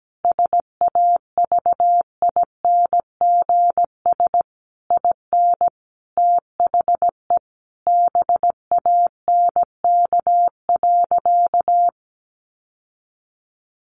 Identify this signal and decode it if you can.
Morse code, heard as a keyed tone, about 17 wpm, 700 Hz. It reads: SAVINGS IN THE BANK.
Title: Sentences from Top 1200 Words 17wpm